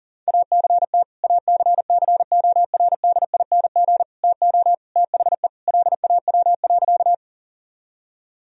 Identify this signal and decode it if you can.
Morse code, heard as a keyed tone, about 40 wpm, 690 Hz. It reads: ACT ACCORDING TO THE LAW.
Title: Sentences from Top 1000 Words - Review Part 4 - 40wpm